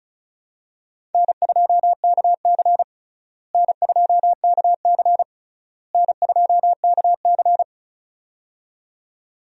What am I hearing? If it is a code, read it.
Morse code, heard as a keyed tone, about 36 wpm, 700 Hz. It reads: N2KC N2KC N2KC
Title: USA Call Signs - 4 Characters - Repeated 3x - 35wpm